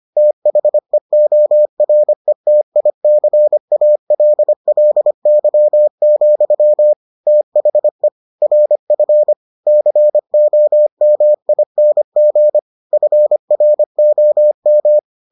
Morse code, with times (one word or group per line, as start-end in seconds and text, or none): 0.2-6.9 THEORETICALLY,
7.3-8.1 THE
8.4-9.3 RF
9.7-12.6 COMING
12.9-15.0 FROM